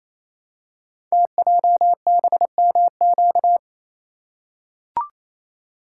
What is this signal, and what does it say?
Morse code, heard as a keyed tone, about 27 wpm, 700 Hz. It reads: TJBMQ
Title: Code Groups - 5 Letters - 28wpm